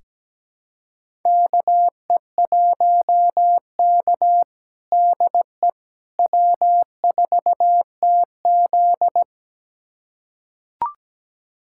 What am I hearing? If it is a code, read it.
Morse code, heard as a keyed tone, about 17 wpm, 700 Hz. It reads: KE1K DE W4TZ
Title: USA Callsign DE Callsign - 4 Characters - 17wpm